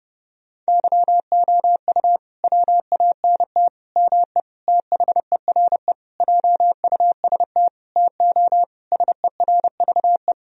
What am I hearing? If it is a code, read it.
Morse code, heard as a keyed tone, about 32 wpm, 700 Hz. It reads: YOU WANT ME THERE JUST TO SERVE
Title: Sentences from Top 1000 Words - Review Part 2 - 30wpm